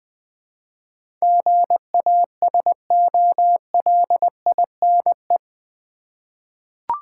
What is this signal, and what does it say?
Morse code, heard as a keyed tone, about 20 wpm, 700 Hz. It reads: GASOLINE
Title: Easy Words with Hints - 20wpm